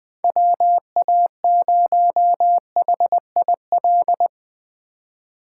Morse code, heard as a keyed tone, about 21 wpm, 700 Hz. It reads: WA0HIL
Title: Speed Racing - USA Call Signs 17wpm